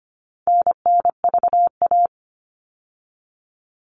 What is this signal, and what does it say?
Morse code, heard as a keyed tone, about 25 wpm, 700 Hz. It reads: NNVA
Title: All Cut Numbers - 4-Digit 25wpm